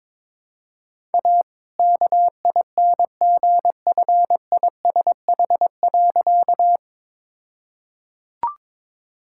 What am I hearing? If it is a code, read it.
Morse code, heard as a keyed tone, about 22 wpm, 700 Hz. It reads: A KINGFISH.
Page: morse-code-ninja-practice-22wpm